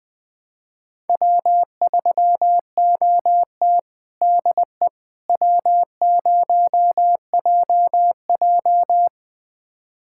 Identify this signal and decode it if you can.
Morse code, heard as a keyed tone, about 20 wpm, 700 Hz. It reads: W3OT DE W0JJ
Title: USA Callsign DE Callsign - 4 Characters - 20wpm